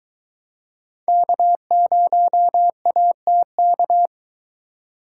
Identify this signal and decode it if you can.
Morse code, heard as a keyed tone, about 22 wpm, 700 Hz. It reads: K0ATK